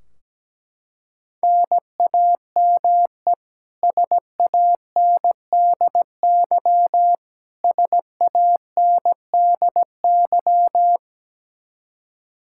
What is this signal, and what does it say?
Morse code, heard as a keyed tone, about 17 wpm, 700 Hz. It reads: NAME SANDY SANDY
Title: K1USN SST QSO Practice 17wpm